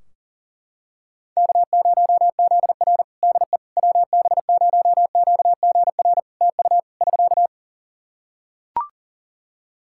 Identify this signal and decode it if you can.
Morse code, heard as a keyed tone, about 40 wpm, 700 Hz. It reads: K0ZR DE WB0QGR TU <SK>
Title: Practice QSO - Exchange 4 - Ending - 40wpm